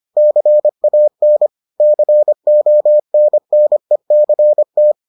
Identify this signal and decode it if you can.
Morse code, heard as a keyed tone, about 25 wpm, 600 Hz. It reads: CAN CONNECT